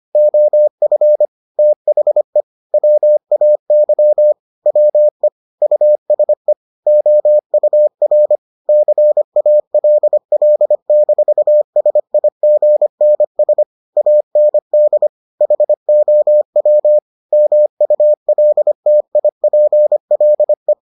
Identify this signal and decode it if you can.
Morse code, heard as a keyed tone, about 25 wpm, 600 Hz. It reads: OF THE WAY WE USE OUR CALL-SIGNS AND HOW MULTIPLE